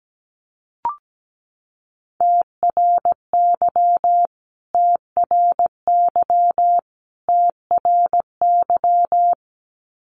Morse code, heard as a keyed tone, about 17 wpm, 700 Hz. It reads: TRY TRY TRY
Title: Top 500 Words Repeated 17wpm